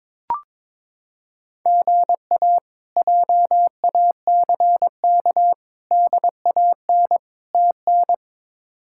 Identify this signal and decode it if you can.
Morse code, heard as a keyed tone, about 22 wpm, 700 Hz. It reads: GA JACK DAN TN
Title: K1USN SST QSO Practice - 22wpm